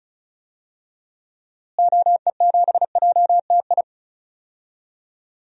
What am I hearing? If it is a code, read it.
Morse code, heard as a keyed tone, about 34 wpm, 700 Hz. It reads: OE7JTI
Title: DXCC-100 Call Signs - 6 Characters - Repeated 3x - 35wpm